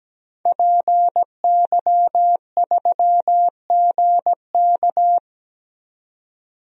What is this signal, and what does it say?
Morse code, heard as a keyed tone, about 17 wpm, 700 Hz. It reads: PY3GK
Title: DXCC-100 Call Signs - 5 Characters - 17wpm